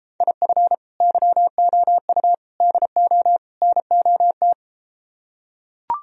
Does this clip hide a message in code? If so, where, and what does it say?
Yes: Morse code from 0.2 to 4.5 s, heard as a keyed tone, about 33 wpm, 700 Hz. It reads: IF YOU DO NOT